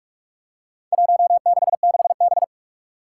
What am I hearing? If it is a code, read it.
Morse code, heard as a keyed tone, about 46 wpm, 700 Hz. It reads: 166B
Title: Four Challenging Letter-Number Combinations 45wpm